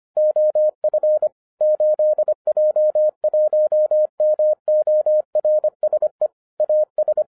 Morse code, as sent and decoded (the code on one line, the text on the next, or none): --- ..-. / ---.. .--- .---- -- --- .-. ... / .- ...
OF 8J1MORSE AS